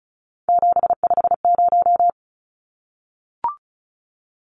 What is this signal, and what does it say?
Morse code, heard as a keyed tone, about 34 wpm, 700 Hz. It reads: 750
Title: Numbers - 3-Digit 35wpm